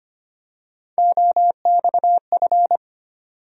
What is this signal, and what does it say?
Morse code, heard as a keyed tone, about 25 wpm, 700 Hz. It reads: OXF
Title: USA Call Sign Postfixes - 3 Letters 25wpm